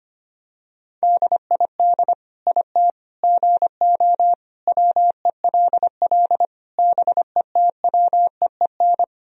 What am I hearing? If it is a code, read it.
Morse code, heard as a keyed tone, about 25 wpm, 700 Hz. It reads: DID IT GO WELL BETWEEN
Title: Sentences from Top 1000 Words - Review Part 1 - 25wpm